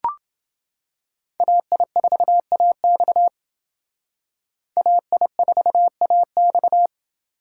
Morse code, heard as a keyed tone, about 28 wpm, 700 Hz. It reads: AI4AX AI4AX